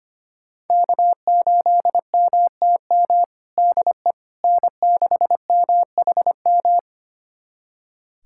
Morse code, heard as a keyed tone, about 25 wpm, 700 Hz. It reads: K8MTM DE N6MHM
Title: USA Callsign DE Callsign - 5 Characters - 25wpm